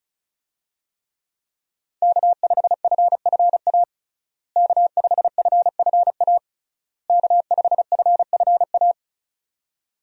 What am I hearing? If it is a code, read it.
Morse code, heard as a keyed tone, about 36 wpm, 700 Hz. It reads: K5FFA K5FFA K5FFA
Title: USA Call Signs - 5 Characters - Repeated 3x - 35wpm